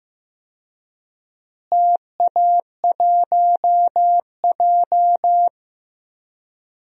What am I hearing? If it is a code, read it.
Morse code, heard as a keyed tone, about 15 wpm, 700 Hz. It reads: TA1J
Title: DXCC-100 Call Signs - 4 Characters - Repeated 3x - 15wpm